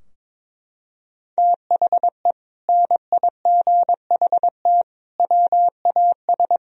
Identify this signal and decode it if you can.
Morse code, heard as a keyed tone, about 22 wpm, 700 Hz. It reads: THE NIGHT WAS
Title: Sentences from Top 1000 Words - Review Part 2 - 22wpm